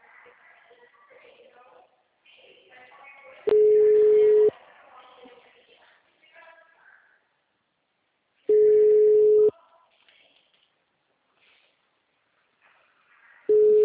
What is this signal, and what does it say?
Ringback tone (Europe, 425 Hz), with speech.